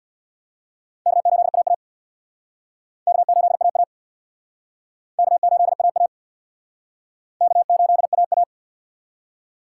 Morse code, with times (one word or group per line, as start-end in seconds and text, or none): 1.1-1.7 K8AA
3.1-3.8 K8AA
5.2-6.1 K8AA
7.4-8.4 K8AA